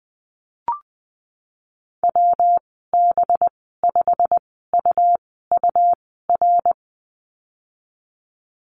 Morse code, heard as a keyed tone, about 20 wpm, 700 Hz. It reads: W B 5 U U R
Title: 238 Lesson 38g - Letter q - Call Signs 20wpm